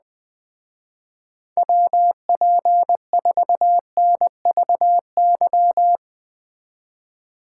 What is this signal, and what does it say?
Morse code, heard as a keyed tone, about 20 wpm, 700 Hz. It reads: WP4NVY
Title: Speed Racing - USA 2x3 Call Signs 17wpm